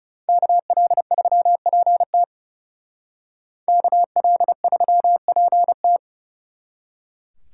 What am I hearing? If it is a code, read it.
Morse code, heard as a keyed tone, about 37 wpm, 700 Hz. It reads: KL3PT KL3PT
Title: Speed Racing - USA 2x2 Call Signs 30wpm